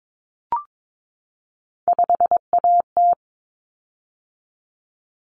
Morse code, heard as a keyed tone, about 22 wpm, 700 Hz. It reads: 5AT
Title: Common Cut Numbers - 3-Digit 22wpm